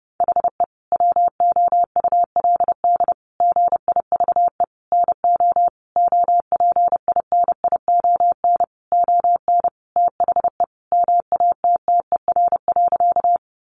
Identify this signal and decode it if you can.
Morse code, heard as a keyed tone, about 30 wpm, 700 Hz. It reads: HE WOULD GIVE NO OPINION ON THE MATTER.
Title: Sentences from Top 1200 Words 30wpm